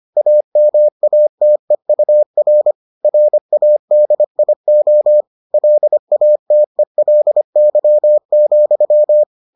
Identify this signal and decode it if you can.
Morse code, heard as a keyed tone, about 25 wpm, 600 Hz. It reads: AMATEUR RADIO LATELY,